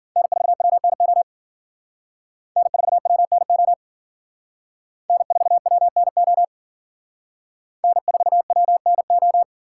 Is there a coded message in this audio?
Yes: Morse code from 0.2 to 9.4 s, heard as a keyed tone, about 51 wpm, 700 Hz. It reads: N4WNO N4WNO N4WNO N4WNO